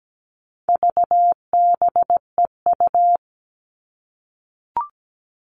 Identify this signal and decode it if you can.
Morse code, heard as a keyed tone, about 17 wpm, 700 Hz. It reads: VBEU
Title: All Cut Numbers - 4-Digit 17wpm